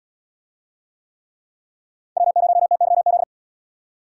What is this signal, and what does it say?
Morse code, heard as a keyed tone, about 74 wpm, 700 Hz. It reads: W0TYZ